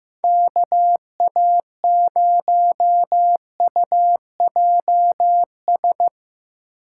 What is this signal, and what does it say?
Morse code, heard as a keyed tone, about 15 wpm, 700 Hz. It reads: KA0UJS
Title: USA Call Signs - 6 Characters - 3x Character Spacing - 15wpm